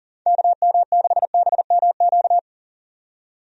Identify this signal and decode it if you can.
Morse code, heard as a keyed tone, about 40 wpm, 700 Hz. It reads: KM6BMQ
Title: USA Call Signs - 6 Characters - 40wpm